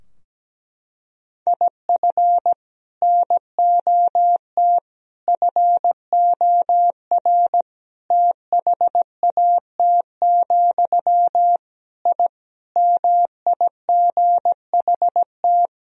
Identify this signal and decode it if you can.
Morse code, heard as a keyed tone, about 17 wpm, 700 Hz. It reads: IF NOT FOR THAT, I MIGHT
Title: Sentences from Top 1000 Words - Review Part 3 - 17wpm